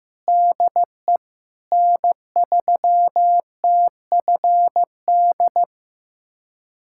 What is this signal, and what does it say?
Morse code, heard as a keyed tone, about 15 wpm, 700 Hz. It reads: DE N3TFD
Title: USA Callsign DE Callsign - 5 Characters - 15wpm